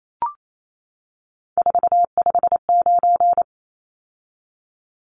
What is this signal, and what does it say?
Morse code, heard as a keyed tone, about 28 wpm, 700 Hz. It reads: 459
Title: Numbers - 3-Digit 28wpm